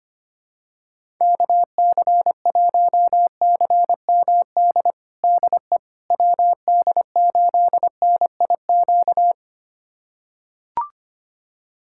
Morse code, heard as a keyed tone, about 25 wpm, 700 Hz. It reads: KC1CMD DE WD8NIQ
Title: USA Callsign DE Callsign 25wpm